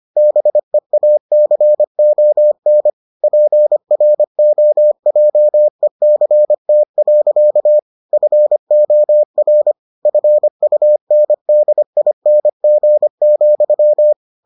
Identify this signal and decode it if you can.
Morse code, heard as a keyed tone, about 25 wpm, 600 Hz. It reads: BEACON PROJECT. FOR FUNDING,